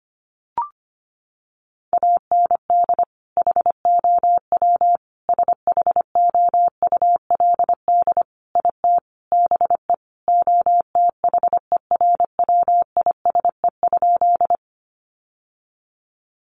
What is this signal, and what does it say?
Morse code, heard as a keyed tone, about 25 wpm, 700 Hz. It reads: AND HOW SHOULD IT BE OTHERWISE?
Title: Sentences from Top 1200 Words 25wpm